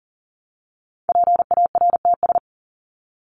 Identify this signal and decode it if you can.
Morse code, heard as a keyed tone, about 40 wpm, 700 Hz. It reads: PARTS
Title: ES:S Suffix - Top 100 Words with Base Words 40wpm